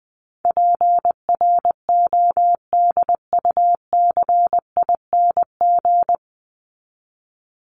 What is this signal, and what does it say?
Morse code, heard as a keyed tone, about 20 wpm, 700 Hz. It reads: PRODUCING
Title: ING Suffix - Top 100 Words with Base Words 20wpm